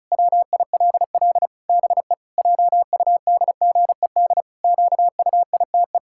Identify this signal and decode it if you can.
Morse code, heard as a keyed tone, about 34 wpm, 700 Hz. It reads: WILL BE JUDGED QUITE
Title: Pangram Challenge v2 - 35wpm